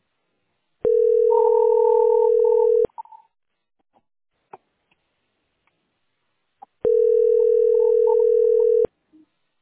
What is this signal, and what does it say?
Ringback tone (North America), 440 + 480 Hz.